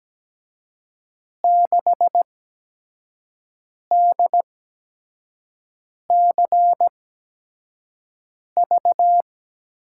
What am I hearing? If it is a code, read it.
Morse code, heard as a keyed tone, about 17 wpm, spaced as for 5 wpm, 700 Hz. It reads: 6DCV
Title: USA Call Signs - 5 Characters - 8x Character Spacing - 17wpm